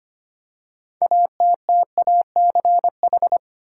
Morse code, heard as a keyed tone, about 25 wpm, 700 Hz. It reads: ATTACH
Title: Six Letter Words - 25wpm